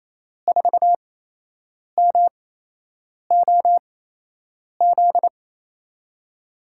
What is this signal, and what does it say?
Morse code, heard as a keyed tone, about 28 wpm, spaced as for 8 wpm, 700 Hz. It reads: 4MOZ
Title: USA Call Signs - 8x Character Spacing - 28wpm